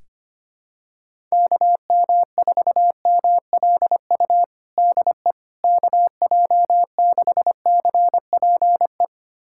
Morse code, heard as a keyed tone, about 25 wpm, 700 Hz. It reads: KM4MLU DE KJ6CPE
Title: USA Callsign DE Callsign 25wpm